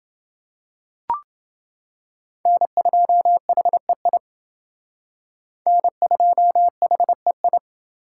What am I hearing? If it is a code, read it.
Morse code, heard as a keyed tone, about 28 wpm, 700 Hz. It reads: N2HEI N2HEI